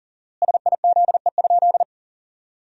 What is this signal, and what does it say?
Morse code, heard as a keyed tone, about 40 wpm, 700 Hz. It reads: SIZE?
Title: 4x Word-Spacing - Sentences from Top 500 Words - 40wpm